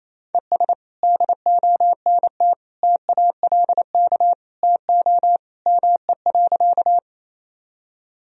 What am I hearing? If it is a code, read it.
Morse code, heard as a keyed tone, about 28 wpm, 700 Hz. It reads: ES DONT TALK TO ME.
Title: Sentences from Top 400 Words and Abbreviations 28wpm